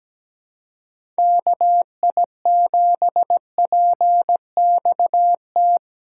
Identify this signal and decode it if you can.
Morse code, heard as a keyed tone, about 17 wpm, 700 Hz. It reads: KI7PXT
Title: USA Call Signs - 2x Character Spacing - 17wpm